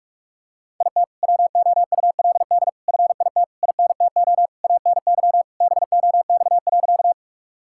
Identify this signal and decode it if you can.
Morse code, heard as a keyed tone, about 46 wpm, 700 Hz. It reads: IT WOULD FIT INTO ANY BOX.
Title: Sentences from Top 1000 Words - Review Part 5 - 45wpm